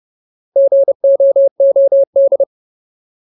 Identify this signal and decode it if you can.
Morse code, heard as a keyed tone, about 30 wpm, 550 Hz. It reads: GOOD
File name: Ultimate Word List / 30WPM Ultimate